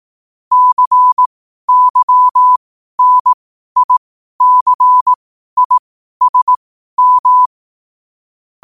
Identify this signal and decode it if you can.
Morse code, heard as a keyed tone, about 18 wpm, spaced as for 12 wpm, 1000 Hz. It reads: CYNICISM
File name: qotdfiles / 12wpm